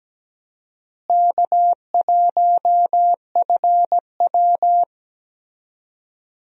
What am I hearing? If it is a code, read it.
Morse code, heard as a keyed tone, about 17 wpm, 700 Hz. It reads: K1FW
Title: USA Call Signs - 4 Characters - Repeated 3x - 17wpm